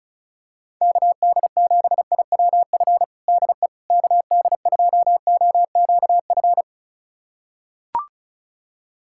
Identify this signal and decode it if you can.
Morse code, heard as a keyed tone, about 36 wpm, 690 Hz. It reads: KD7IWF DE KD2OQF
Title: USA Callsign DE Callsign - 6 Characters - 35wpm